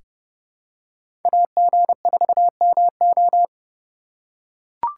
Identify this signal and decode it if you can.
Morse code, heard as a keyed tone, about 31 wpm, 700 Hz. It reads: AG4MO
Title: USA Call Signs - 5 Characters - 2x Character Spacing - 30wpm